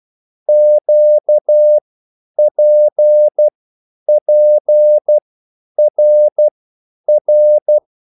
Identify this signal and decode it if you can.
Morse code, heard as a keyed tone, about 12 wpm, spaced as for 9 wpm, 600 Hz. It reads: QPPRR
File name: cwbook